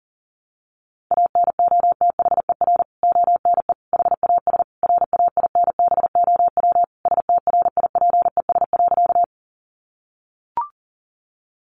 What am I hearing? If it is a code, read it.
Morse code, heard as a keyed tone, about 40 wpm, 700 Hz. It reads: ANOTHER ONE HAS RAINBOW STRIPES.
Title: Additional Easy Phrases 40wpm